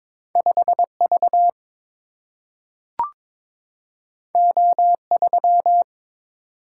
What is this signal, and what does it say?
Morse code, heard as a keyed tone, about 22 wpm, 700 Hz. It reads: 5 V O 3
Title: Single-Letter Single-Number 22wpm